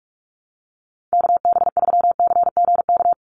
Morse code, heard as a keyed tone, about 44 wpm, 700 Hz. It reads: KB3CGK
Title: USA Call Signs 45wpm